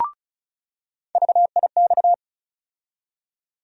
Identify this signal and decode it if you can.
Morse code, heard as a keyed tone, about 36 wpm, 700 Hz. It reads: VIX
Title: USA Call Sign Postfixes - 3 Letters 35wpm